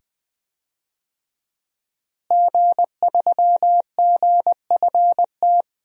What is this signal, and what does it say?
Morse code, heard as a keyed tone, about 20 wpm, 700 Hz. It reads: G3GFT
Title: DXCC-100 Call Signs - 5 Characters - Repeated 2x - 20wpm